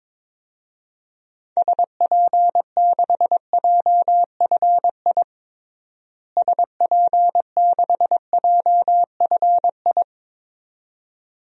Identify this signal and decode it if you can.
Morse code, heard as a keyed tone, about 22 wpm, 700 Hz. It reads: SP6JFI SP6JFI
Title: DXCC-100 Call Signs - 6 Characters - Repeated 2x - 22wpm